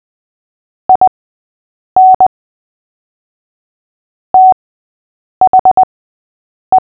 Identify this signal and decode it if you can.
Morse code, heard as a keyed tone, about 20 wpm, spaced as for 8 wpm, 750 Hz. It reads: IN THE
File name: cw / fists / Short-Story-6-8wpm